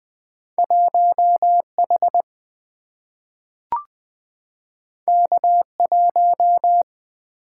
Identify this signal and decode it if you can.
Morse code, heard as a keyed tone, about 20 wpm, 690 Hz. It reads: 1H K1
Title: Two Letter-Number Combinations - 20wpm